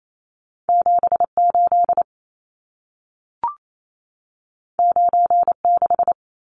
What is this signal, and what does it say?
Morse code, heard as a keyed tone, about 28 wpm, 700 Hz. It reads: 78 96
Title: Numbers - 2-Digit 28wpm